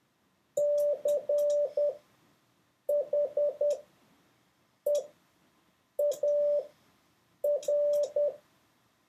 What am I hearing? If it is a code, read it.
Morse code, heard as a keyed tone, about 10 wpm, spaced as for 6 wpm, 580 Hz. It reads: CHEAR